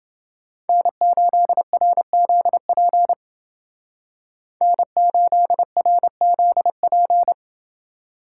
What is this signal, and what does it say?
Morse code, heard as a keyed tone, about 28 wpm, 700 Hz. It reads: N8RZP N8RZP